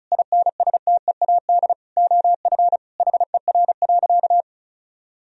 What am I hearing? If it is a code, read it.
Morse code, heard as a keyed tone, about 36 wpm, 700 Hz. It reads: INSTEAD OF HER.
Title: Sentences from Top 1000 Words - Review Part 4 - 35wpm